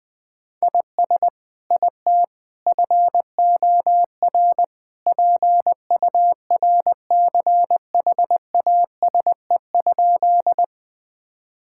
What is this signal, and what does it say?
Morse code, heard as a keyed tone, about 20 wpm, 700 Hz. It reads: IS IT FOR PURCHASE?
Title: Sentences from Top 1000 Words - Review Part 5 - 20wpm